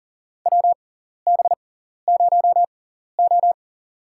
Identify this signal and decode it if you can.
Morse code, heard as a keyed tone, about 40 wpm, spaced as for 14 wpm, 700 Hz. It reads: WB0O
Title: USA Call Signs - 4 Characters - 6x Character Spacing - 40wpm